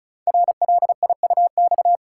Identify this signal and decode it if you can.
Morse code, heard as a keyed tone, about 36 wpm, 700 Hz. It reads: RLIUX
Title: Code Groups - 5 Letter:Numbers - 35wpm